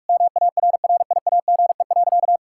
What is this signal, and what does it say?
Morse code, heard as a keyed tone, about 41 wpm, 700 Hz. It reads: MARRIAGE.
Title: Sentences from Top 1000 Words - Review Part 6 - 45wpm